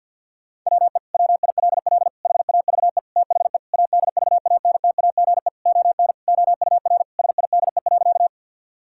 Morse code, heard as a keyed tone, about 50 wpm, 700 Hz. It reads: WE WILL HAVE THE ADVANTAGE ON OUR SIDE.
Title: Sentences from Top 1000 Words - Review Part 6 - 50wpm